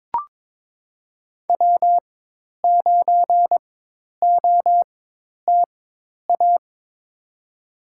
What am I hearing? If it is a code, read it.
Morse code, heard as a keyed tone, about 23 wpm, spaced as for 10 wpm, 700 Hz. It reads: W9OTA
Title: USA Call Signs - 5 Characters - 4x Character Spacing - 22wpm